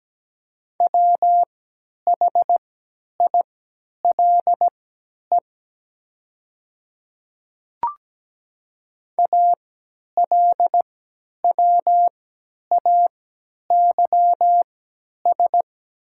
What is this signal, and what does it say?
Morse code, heard as a keyed tone, about 17 wpm, spaced as for 10 wpm, 700 Hz. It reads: WHILE ALWAYS
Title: 3x Character Spacing - Top 500 Words - 17wpm